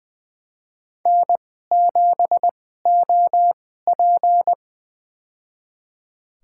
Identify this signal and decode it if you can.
Morse code, heard as a keyed tone, about 21 wpm, spaced as for 15 wpm, 700 Hz. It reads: N7OP
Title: USA Call Signs - 4 Characters - 2x Character Spacing - 20wpm